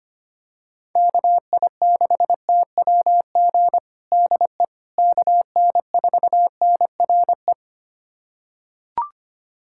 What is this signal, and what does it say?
Morse code, heard as a keyed tone, about 25 wpm, 700 Hz. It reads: KI6TWG DE KN4NRE
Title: USA Callsign DE Callsign - 6 Characters - 25wpm